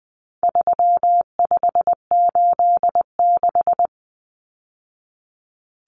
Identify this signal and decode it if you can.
Morse code, heard as a keyed tone, about 20 wpm, 700 Hz. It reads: 3586
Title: Numbers - 4-Digit 20wpm